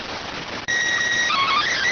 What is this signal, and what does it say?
SSTV (slow-scan television), PD 290.